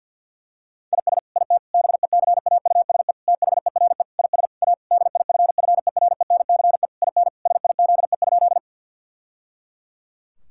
Morse code, heard as a keyed tone, about 50 wpm, 700 Hz. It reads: IS IT BECAUSE THERE IS A DIFFERENCE IN SIZE?
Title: Sentences from Top 1000 Words - Review Part 5 - 50wpm